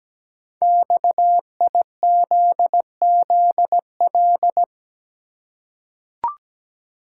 Morse code, heard as a keyed tone, about 17 wpm, 700 Hz. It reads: XIZZL
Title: Code Groups - 5 Letters - 17wpm